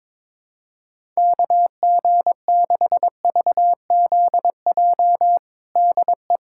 Morse code, heard as a keyed tone, about 21 wpm, 700 Hz. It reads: KG6VZJ DE
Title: USA Callsign DE Callsign - 6 Characters - 22wpm